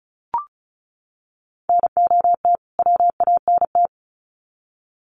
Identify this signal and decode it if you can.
Morse code, heard as a keyed tone, about 34 wpm, 700 Hz. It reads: NOT WANT
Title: Sets of 2 Words 35wpm